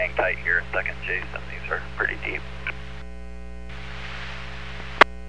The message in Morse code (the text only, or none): none